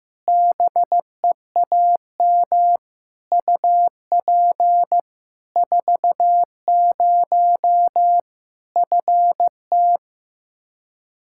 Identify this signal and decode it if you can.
Morse code, heard as a keyed tone, about 15 wpm, 700 Hz. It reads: BEAM UP 40 FT
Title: K1USN SST QSO Practice 15wpm